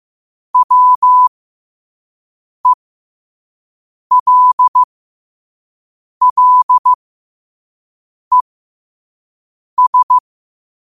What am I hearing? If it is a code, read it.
Morse code, heard as a keyed tone, about 15 wpm, spaced as for 5 wpm, 1000 Hz. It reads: WELLES
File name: qotdfiles / 5wpm